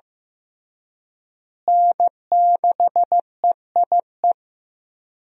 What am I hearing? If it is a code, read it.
Morse code, heard as a keyed tone, about 15 wpm, 700 Hz. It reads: N6EIE
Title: USA Call Signs - 5 Characters - Repeated 3x - 15wpm